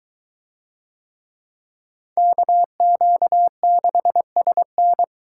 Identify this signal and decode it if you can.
Morse code, heard as a keyed tone, about 22 wpm, 700 Hz. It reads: KQ6SN